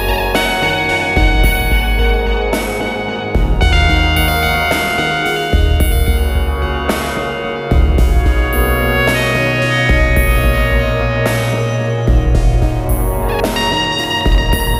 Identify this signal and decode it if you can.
Dial-up modem handshake.